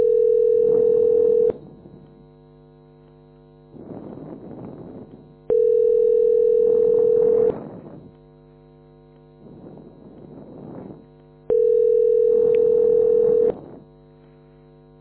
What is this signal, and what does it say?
Ringback tone (North America), 440 + 480 Hz.
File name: f80dc860-67e6-4459-8175